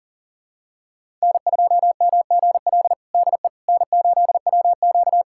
Morse code, heard as a keyed tone, about 40 wpm, 700 Hz. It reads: N2MGL DE N8WQ
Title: USA Callsign DE Callsign 40wpm